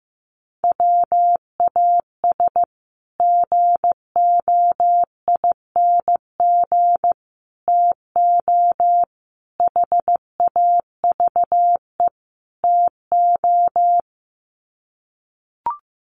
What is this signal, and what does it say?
Morse code, heard as a keyed tone, about 15 wpm, 700 Hz. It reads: WAS GOING TO HAVE TO
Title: Sets of 5 Words Encore 15wpm